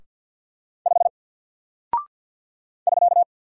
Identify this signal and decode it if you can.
Morse code, heard as a keyed tone, about 48 wpm, 700 Hz. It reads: 5 <SK>
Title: Single Character (Letter:Number:Punctuation:Prosign) Alt 50wpm